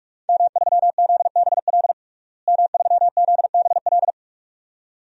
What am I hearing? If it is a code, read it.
Morse code, heard as a keyed tone, about 43 wpm, 700 Hz. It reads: M3ZBL M3ZBL
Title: DXCC-100 Call Signs - 5 Characters - Repeated 2x - 45wpm